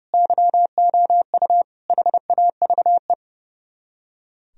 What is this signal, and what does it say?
Morse code, heard as a keyed tone, about 36 wpm, 700 Hz. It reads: YOU HAVE
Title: Speed-Racing - Sets of 3 Words 30wpm